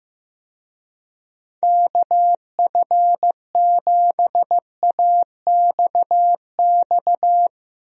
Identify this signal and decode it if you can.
Morse code, heard as a keyed tone, about 15 wpm, 700 Hz. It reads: KF7AXX
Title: USA Call Signs - 6 Characters - 15wpm